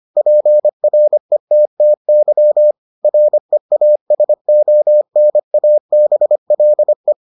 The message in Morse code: PRETTY REASONABLE

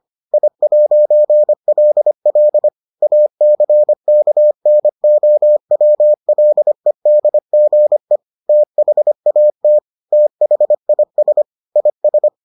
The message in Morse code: I'LL ACKNOWLEDGE THAT THIS IS

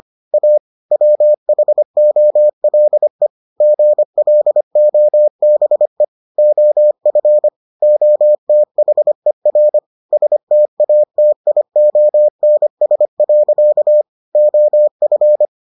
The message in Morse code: A WHOLE GLOBE OF OTHER STATIONS. OF

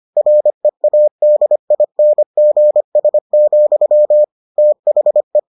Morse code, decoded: READINGS, THE

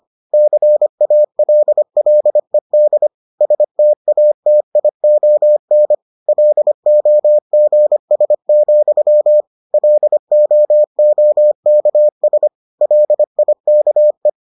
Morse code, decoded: CALLED STATION LOGS, LOOKS LIKE

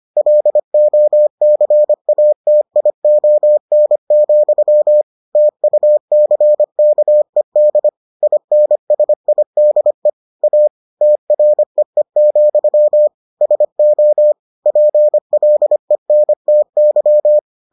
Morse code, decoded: LOCATION, TUCKED INSIDE A TREE, SO PLENTY

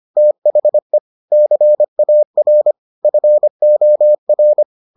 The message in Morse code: THE CAR FOR

